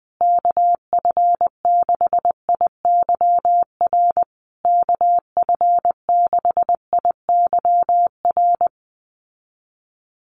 Morse code, heard as a keyed tone, about 19 wpm, 700 Hz. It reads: KF6IYR KF6IYR
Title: USA Call Signs Repeated 3x 20wpm